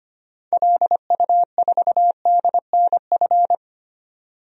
Morse code, heard as a keyed tone, about 25 wpm, 700 Hz. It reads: LU4DNF
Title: DXCC-100 Call Signs - 6 Characters - 25wpm